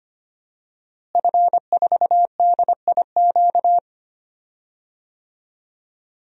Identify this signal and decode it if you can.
Morse code, heard as a keyed tone, about 25 wpm, 700 Hz. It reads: F4DIQ
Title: DXCC-100 Call Signs - 5 Characters - Repeated 3x - 25wpm